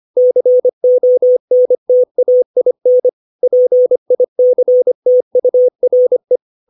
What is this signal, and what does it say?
Morse code, heard as a keyed tone, about 25 wpm, 500 Hz. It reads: CONTAIN PICTURE
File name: 632 words / 632 Words 25WPM